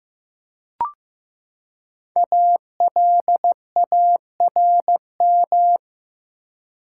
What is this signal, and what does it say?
Morse code, heard as a keyed tone, about 15 wpm, 700 Hz. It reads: ALARM
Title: Five Letter Words - 15wpm